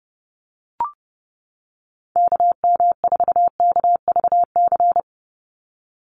Morse code, heard as a keyed tone, about 30 wpm, 700 Hz. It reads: KM4KVC
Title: USA Call Signs 30wpm